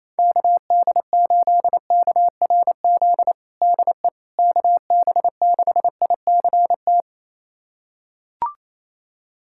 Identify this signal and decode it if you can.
Morse code, heard as a keyed tone, about 27 wpm, 700 Hz. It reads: KD8KRZ DE KB6ICT
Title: USA Callsign DE Callsign 28wpm